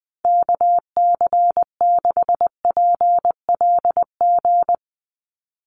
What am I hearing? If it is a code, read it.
Morse code, heard as a keyed tone, about 19 wpm, 700 Hz. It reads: KC6PLG
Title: USA Call Signs 20wpm